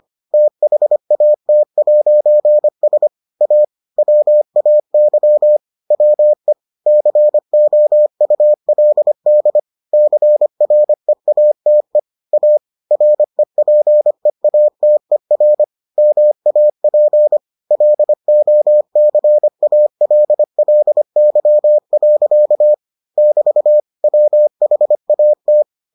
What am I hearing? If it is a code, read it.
Morse code, heard as a keyed tone, about 25 wpm, 600 Hz. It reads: THAT'S A WAY WE COULD CREATE A REPEATER MAP LOCALLY. <BT> WHAT